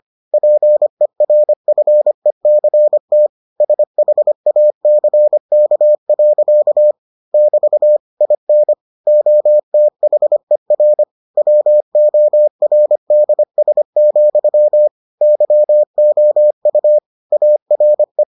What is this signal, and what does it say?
Morse code, heard as a keyed tone, about 25 wpm, 600 Hz. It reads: PERFECT SHACK. <BT> IN OTHER WORDS, YOU ARE